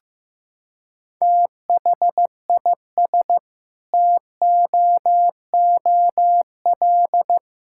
Morse code, heard as a keyed tone, about 15 wpm, 700 Hz. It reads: THIS TOOL
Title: Sentences from Top 1000 Words - Review Part 6 - 15wpm